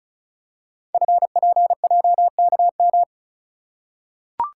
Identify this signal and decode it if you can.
Morse code, heard as a keyed tone, about 36 wpm, 700 Hz. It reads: FPJKM
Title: Code Groups - 5 Letter:Numbers - 35wpm